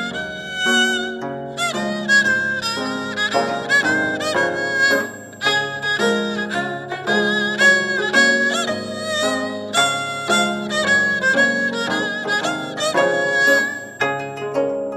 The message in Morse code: none